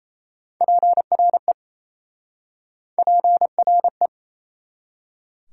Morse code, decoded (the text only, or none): PRE PRE